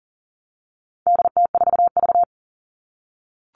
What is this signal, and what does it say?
Morse code, heard as a keyed tone, about 40 wpm, 700 Hz. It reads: DT4V
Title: All Cut Numbers - 4-Digit 40wpm